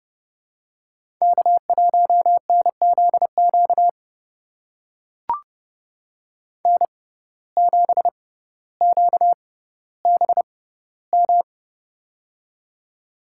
Morse code, heard as a keyed tone, about 30 wpm, spaced as for 15 wpm, 700 Hz. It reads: K1NZQ N7QBM
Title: USA Call Signs - 5 Characters - 6x Character Spacing - 30wpm